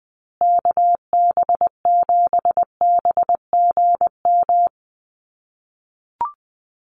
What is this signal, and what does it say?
Morse code, heard as a keyed tone, about 20 wpm, 700 Hz. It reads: KB7BGM
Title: USA Call Signs Repeated 3x 20wpm